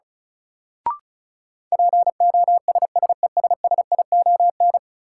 Morse code, heard as a keyed tone, about 36 wpm, 700 Hz. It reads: POSSESSION
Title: Sentences from Top 1400 Words - New Word Drill Alt - 35wpm